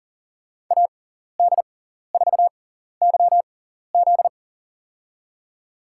Morse code, heard as a keyed tone, about 37 wpm, spaced as for 14 wpm, 700 Hz. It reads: AD4YZ
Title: USA Call Signs - 5 Characters - 6x Character Spacing - 40wpm